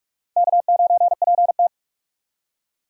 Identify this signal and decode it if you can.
Morse code, heard as a keyed tone, about 44 wpm, 700 Hz. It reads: K9PT